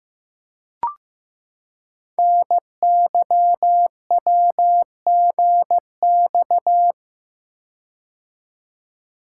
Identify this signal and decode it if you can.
Morse code, heard as a keyed tone, about 15 wpm, 700 Hz. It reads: NYWGX
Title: Code Groups - 5 Letter:Numbers - 15wpm